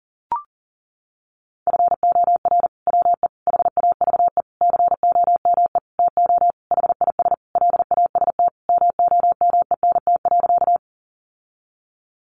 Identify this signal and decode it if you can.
Morse code, heard as a keyed tone, about 40 wpm, 700 Hz. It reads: FOR WE HAVE COME TO HIS LAST MOMENT.
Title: Sentences from Top 1000 Words 40wpm